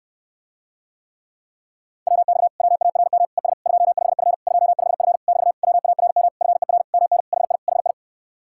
Morse code, heard as a keyed tone, about 74 wpm, 700 Hz. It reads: WX WARM ES 25F 25F <BT> WARM FER MN HI HI